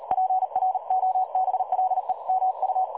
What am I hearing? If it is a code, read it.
Morse code, heard as a keyed tone, about 37 wpm, 750 Hz. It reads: PROBLEMU
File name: dlazaawansowanych / nagraniazpasma / websdr recording 3565.0kHz